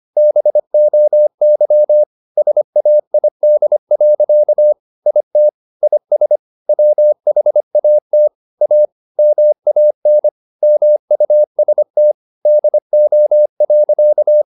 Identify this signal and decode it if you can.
Morse code, heard as a keyed tone, about 25 wpm, 600 Hz. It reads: BOY SAID. IT IS WHAT A MAN MUST DO.
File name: the old man and the sea part 1 25wpm